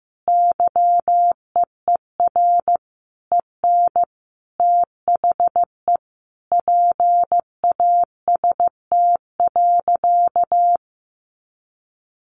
Morse code, heard as a keyed tone, about 15 wpm, 700 Hz. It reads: YEER EN THE PAST.
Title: A Sound Of Thunder 15wpm